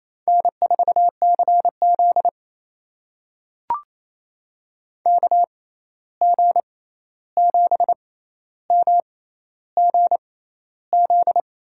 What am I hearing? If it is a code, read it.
Morse code, heard as a keyed tone, about 28 wpm, spaced as for 13 wpm, 700 Hz. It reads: N4CZ KG7MGZ
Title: USA Call Signs - 6x Character Spacing - 28wpm